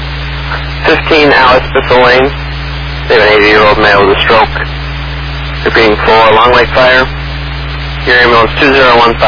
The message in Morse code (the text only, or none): none